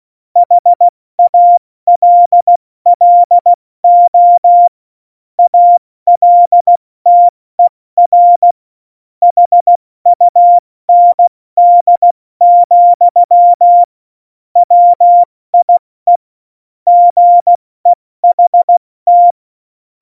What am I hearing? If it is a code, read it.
Morse code, heard as a keyed tone, about 16 wpm, 700 Hz. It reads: HALLO ALTER HUND, WIE GEHT
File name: Morsekurs / Witze / Witz 06. Oktober 2020, 70 F80